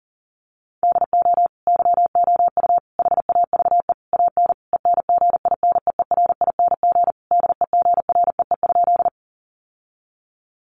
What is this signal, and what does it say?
Morse code, heard as a keyed tone, about 40 wpm, 700 Hz. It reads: DO YOU HAVE AN ENGINEERING DEGREE?
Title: Sentences from Top 1500 Words 40wpm